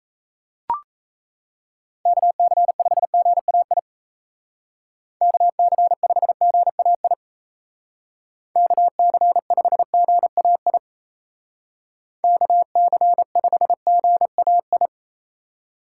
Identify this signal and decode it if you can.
Morse code, heard as a keyed tone, about 34 wpm, 700 Hz. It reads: KC5GAI KC5GAI KC5GAI KC5GAI
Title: Speed Racing - USA 2x3 Call Signs 28wpm